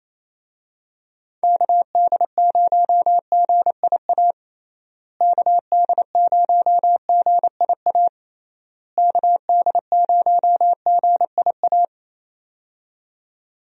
Morse code, heard as a keyed tone, about 28 wpm, 700 Hz. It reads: KD0GIA KD0GIA KD0GIA
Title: USA Call Signs - 6 Characters - Repeated 3x - 28wpm